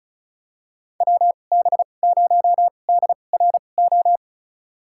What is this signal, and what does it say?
Morse code, heard as a keyed tone, about 34 wpm, spaced as for 25 wpm, 700 Hz. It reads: WB0DRO
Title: USA Call Signs - 2x Character Spacing - 35wpm